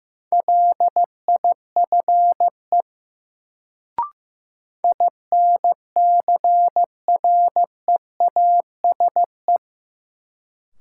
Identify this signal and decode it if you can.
Morse code, heard as a keyed tone, about 16 wpm, 700 Hz. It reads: LIFE INCREASE
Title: Sets of 5 Words - New Word Drill - 15wpm